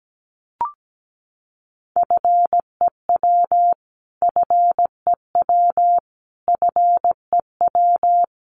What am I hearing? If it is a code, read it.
Morse code, heard as a keyed tone, about 17 wpm, 700 Hz. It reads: FEW FEW FEW
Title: Top 500 Words Repeated 17wpm